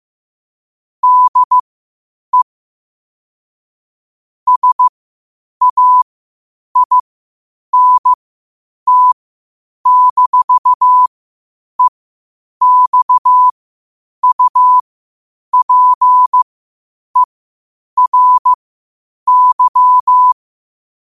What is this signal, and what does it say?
Morse code, heard as a keyed tone, about 15 wpm, spaced as for 8 wpm, 1000 Hz. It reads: DE SAINT-EXUPERY
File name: qotdfiles / 8wpm